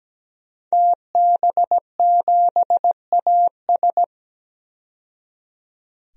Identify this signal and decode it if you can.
Morse code, heard as a keyed tone, about 17 wpm, 700 Hz. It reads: TB7AS
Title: DXCC-100 Call Signs - 5 Characters - 17wpm